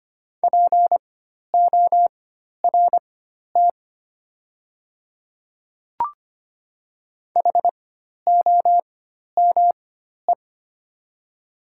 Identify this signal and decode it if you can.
Morse code, heard as a keyed tone, about 25 wpm, spaced as for 12 wpm, 690 Hz. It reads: PORT HOME
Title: 4x Character Spacing - Top 500 Words - 25wpm